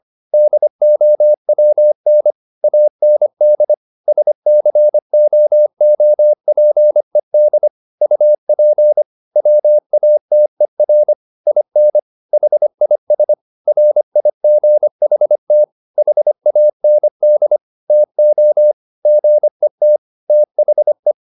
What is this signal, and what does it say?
Morse code, heard as a keyed tone, about 25 wpm, 600 Hz. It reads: DOWN AND SCOOPED UP WATER IN HIS RIGHT HAND TO GET THE